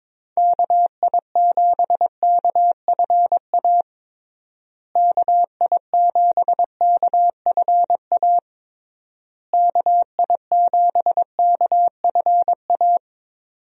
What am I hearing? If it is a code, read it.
Morse code, heard as a keyed tone, about 22 wpm, 700 Hz. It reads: KI7KFA KI7KFA KI7KFA
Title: USA Call Signs - 6 Characters - Repeated 3x - 22wpm